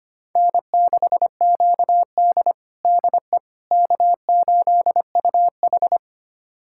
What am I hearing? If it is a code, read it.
Morse code, heard as a keyed tone, about 25 wpm, 700 Hz. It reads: N6QD DE K8UH
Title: USA Callsign DE Callsign - 4 Characters - 25wpm